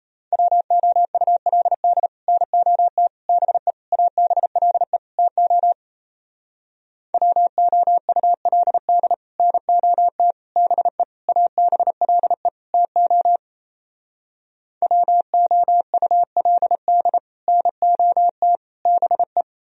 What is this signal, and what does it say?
Morse code, heard as a keyed tone, about 38 wpm, 700 Hz. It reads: WOULD NOT BE ABLE TO WOULD NOT BE ABLE TO WOULD NOT BE